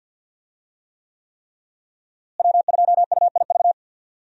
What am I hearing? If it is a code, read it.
Morse code, heard as a keyed tone, about 52 wpm, 700 Hz. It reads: W2UIV
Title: USA Call Signs - 5 Characters - Repeated 3x - 50wpm